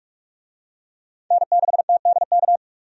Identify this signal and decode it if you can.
Morse code, heard as a keyed tone, about 44 wpm, 700 Hz. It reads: N6TDK
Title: USA Call Signs - 5 Characters - 45wpm